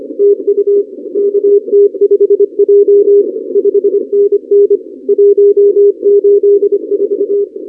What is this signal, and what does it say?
Morse code, heard as a keyed tone, about 26 wpm, 400 Hz. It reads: TU KT5J 5NN 184
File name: contests / 2019-cw-wpx-kt5j-k5pi- -2330 0-31